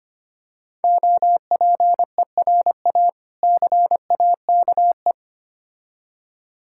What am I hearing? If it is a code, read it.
Morse code, heard as a keyed tone, about 25 wpm, 700 Hz. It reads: OPERA CAKE
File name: practice / 25wpm / Difficult Words with Hints - 25wpm